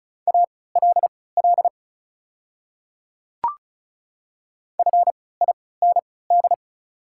Morse code, heard as a keyed tone, about 36 wpm, spaced as for 20 wpm, 690 Hz. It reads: ALL FIND
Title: 3x Character Spacing - Top 100 Words - 35wpm